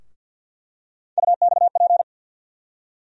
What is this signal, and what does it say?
Morse code, heard as a keyed tone, about 51 wpm, 700 Hz. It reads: UXP
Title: USA Call Sign Postfixes - 3 Letters 50wpm